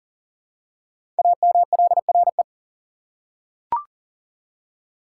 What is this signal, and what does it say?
Morse code, heard as a keyed tone, about 39 wpm, 700 Hz. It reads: AMLRE
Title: Code Groups - 5 Letters - 40wpm